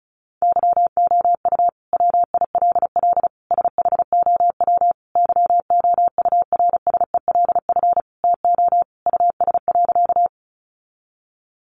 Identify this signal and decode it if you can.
Morse code, heard as a keyed tone, about 34 wpm, 700 Hz. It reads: YOU WILL SHOW YOURSELF TO US.
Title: Sentences from Top 600 Words 35wpm